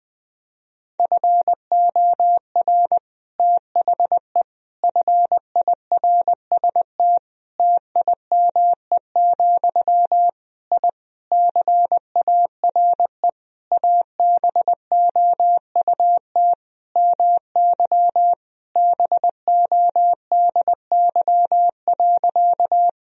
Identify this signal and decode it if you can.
Morse code, heard as a keyed tone, about 20 wpm, 700 Hz. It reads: FOR THE FIRST TIME, I CARE ABOUT MY BODY.
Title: Sentences from Top 1000 Words - Review Part 2 - 20wpm